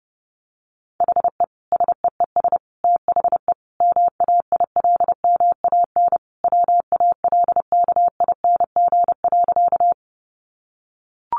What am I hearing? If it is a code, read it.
Morse code, heard as a keyed tone, about 30 wpm, 700 Hz. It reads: HE SEES THE MAILMAN WALKING.
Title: Easy Phrases 30wpm